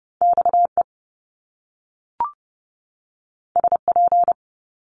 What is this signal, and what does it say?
Morse code, heard as a keyed tone, about 30 wpm, 700 Hz. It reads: XE SP
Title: Top Two Letter Combinations in Words 30wpm